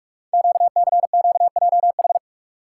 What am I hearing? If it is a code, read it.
Morse code, heard as a keyed tone, about 44 wpm, 700 Hz. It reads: QCQJH
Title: Code Groups - 5 Letters - 45wpm